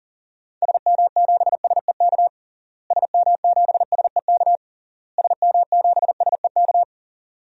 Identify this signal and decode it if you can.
Morse code, heard as a keyed tone, about 40 wpm, 700 Hz. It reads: SM7SEK SM7SEK SM7SEK
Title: DXCC-100 Call Signs - 6 Characters - Repeated 3x - 40wpm